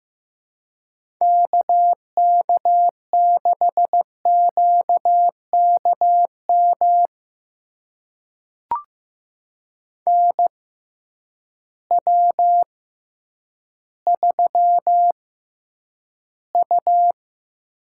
Morse code, heard as a keyed tone, about 15 wpm, spaced as for 8 wpm, 700 Hz. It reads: K K6QKM NW3U
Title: USA Call Signs - 6x Character Spacing - 15wpm